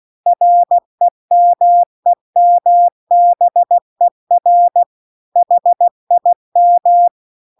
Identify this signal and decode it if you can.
Morse code, heard as a keyed tone, about 16 wpm, 700 Hz. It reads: REMEMBER HIM